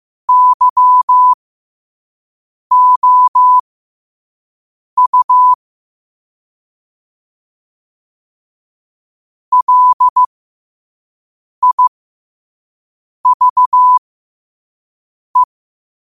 Morse code, heard as a keyed tone, about 15 wpm, spaced as for 5 wpm, 1000 Hz. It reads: YOU LIVE
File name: qotdfiles / 5wpm